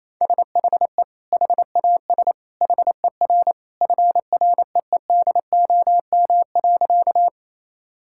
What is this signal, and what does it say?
Morse code, heard as a keyed tone, about 28 wpm, 700 Hz. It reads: SHE HAS HER FREEDOM.
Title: Sentences from Top 1000 Words - Review Part 5 - 28wpm